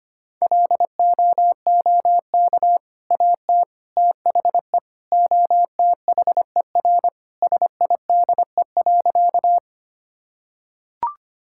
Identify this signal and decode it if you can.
Morse code, heard as a keyed tone, about 25 wpm, 700 Hz. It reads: LOOK AT THE OTHER SIDE.
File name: practice / 25wpm / Sentences with 2-Word Phrases - 25wpm